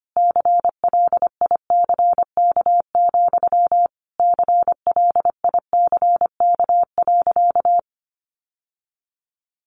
Morse code, heard as a keyed tone, about 25 wpm, 700 Hz. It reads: CLICK, CLICK.
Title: Sentences from Top 1100 Words 25wpm